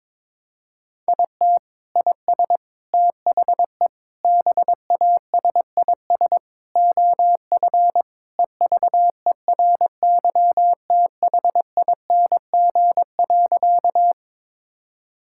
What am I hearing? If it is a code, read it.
Morse code, heard as a keyed tone, about 22 wpm, 700 Hz. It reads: IT IS THE BASIS OF EVERYTHING.
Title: Sentences from Top 1000 Words - Review Part 5 - 22wpm